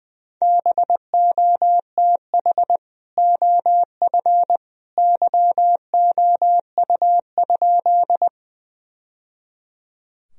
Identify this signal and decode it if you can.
Morse code, heard as a keyed tone, about 20 wpm, 700 Hz. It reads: BOTH OF YOU?
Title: Sentences from Top 1000 Words - Review Part 1 - 20wpm